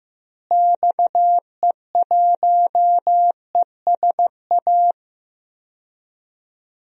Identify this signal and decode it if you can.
Morse code, heard as a keyed tone, about 15 wpm, 700 Hz. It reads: XE1ESA
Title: DXCC-100 Call Signs - 6 Characters - Repeated 2x - 15wpm